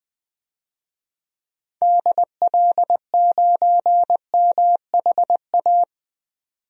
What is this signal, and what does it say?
Morse code, heard as a keyed tone, about 20 wpm, 700 Hz. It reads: DL9MHA
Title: DXCC-100 Call Signs - 6 Characters - 20wpm